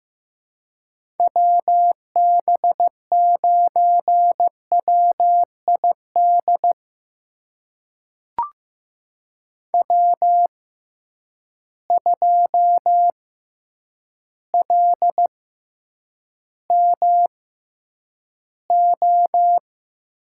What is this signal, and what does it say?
Morse code, heard as a keyed tone, about 15 wpm, spaced as for 8 wpm, 700 Hz. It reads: WB9WID W2LMO
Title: USA Call Signs - 6x Character Spacing - 15wpm